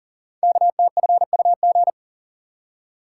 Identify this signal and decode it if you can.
Morse code, heard as a keyed tone, about 40 wpm, 700 Hz. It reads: KTFUG
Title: Code Groups - 5 Letters - 40wpm